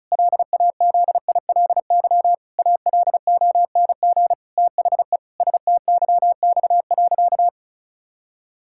Morse code, heard as a keyed tone, about 36 wpm, 700 Hz. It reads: LAZILY ALONG THE STYX.
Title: Pangram Challenge v2 - 35wpm